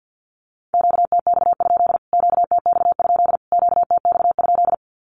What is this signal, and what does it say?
Morse code, heard as a keyed tone, about 50 wpm, 700 Hz. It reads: QTX? QTX? QTX?
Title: Q-Codes Repeated 3x 50wpm